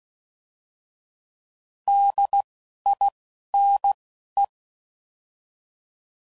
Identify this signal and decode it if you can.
Morse code, heard as a keyed tone, about 16 wpm, spaced as for 11 wpm, 800 Hz. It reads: DINE
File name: Prefixes / By First Letter / Ds